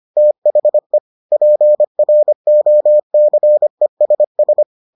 Morse code, heard as a keyed tone, about 25 wpm, 600 Hz. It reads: THE PROCESS